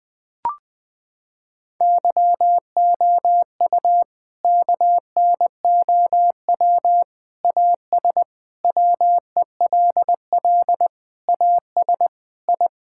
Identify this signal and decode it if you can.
Morse code, heard as a keyed tone, about 20 wpm, 700 Hz. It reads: YOU KNOW AS WELL AS I
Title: Sentences with 8-Word Phrases - 20wpm